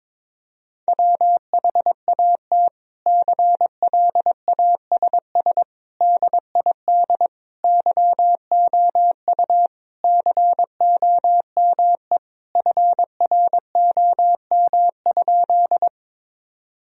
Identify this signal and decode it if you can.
Morse code, heard as a keyed tone, about 22 wpm, 700 Hz. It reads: WHAT CLASS DID YOU COME FROM?
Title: Sentences from Top 1000 Words - Review Part 2 - 22wpm